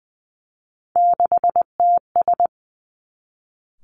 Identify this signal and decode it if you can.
Morse code, heard as a keyed tone, about 20 wpm, 700 Hz. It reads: 6TS
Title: Three Challenging Letter-Number Combinations 20wpm